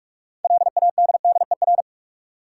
Morse code, heard as a keyed tone, about 43 wpm, 700 Hz. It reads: LADDER
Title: Six Letter Words - 45wpm